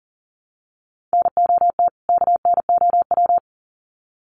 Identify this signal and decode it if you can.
Morse code, heard as a keyed tone, about 40 wpm, 700 Hz. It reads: NOT KNOW
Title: Sets of 2 Words 40wpm